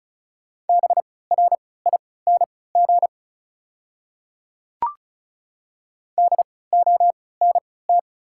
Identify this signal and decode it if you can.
Morse code, heard as a keyed tone, about 34 wpm, spaced as for 20 wpm, 690 Hz. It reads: BRING DONT
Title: 3x Character Spacing - Top 500 Words - 35wpm